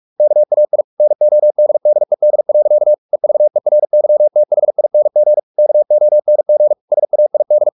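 Morse code, heard as a keyed tone, about 44 wpm, 600 Hz. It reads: KAI NODDED. EVERYTHING KONG SAID
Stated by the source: Morse practice at 45 wpm, the novel Never by Ken Follett in English, from CW med Gnister.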